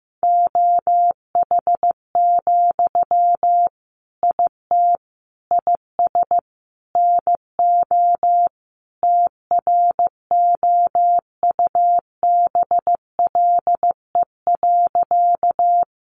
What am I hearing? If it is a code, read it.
Morse code, heard as a keyed tone, about 15 wpm, 700 Hz. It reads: OH, IT IS NO TROUBLE.
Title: Sentences from Top 1200 Words 15wpm